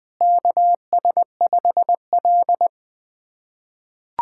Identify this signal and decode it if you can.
Morse code, heard as a keyed tone, about 20 wpm, 700 Hz. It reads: KS5L
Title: USA Callsign DE Callsign - 4 Characters - 20wpm